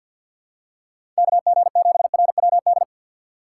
Morse code, heard as a keyed tone, about 47 wpm, 700 Hz. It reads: KG7RWD
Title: USA Call Signs - 2x Character Spacing - 50wpm